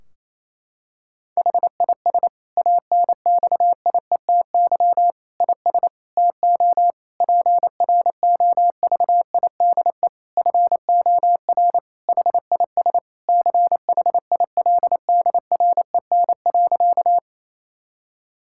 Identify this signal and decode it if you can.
Morse code, heard as a keyed tone, about 28 wpm, 700 Hz. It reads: HIS ANXIETY IS TO PROVIDE FOR HIS CHILDREN.
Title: Sentences from Top 1600 Words 28wpm